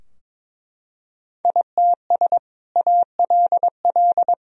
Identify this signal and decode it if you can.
Morse code, heard as a keyed tone, about 22 wpm, 700 Hz. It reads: ITS ALL
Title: Sentences from Top 1000 Words - Review Part 3 - 22wpm